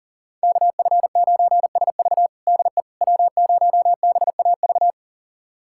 Morse code, heard as a keyed tone, about 40 wpm, 700 Hz. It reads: KF9SV DE W0BAV
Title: USA Callsign DE Callsign - 5 Characters - 40wpm